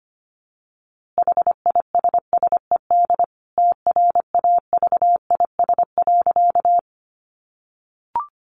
Morse code, heard as a keyed tone, about 25 wpm, 700 Hz. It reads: HISSED TRAVIS.
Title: A Sound Of Thunder 25wpm